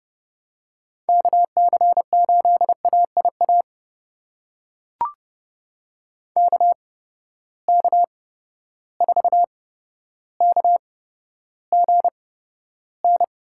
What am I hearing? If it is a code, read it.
Morse code, heard as a keyed tone, about 30 wpm, spaced as for 13 wpm, 700 Hz. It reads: KC8AIA KK4KGN
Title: USA Call Signs - 6 Characters - 8x Character Spacing - 30wpm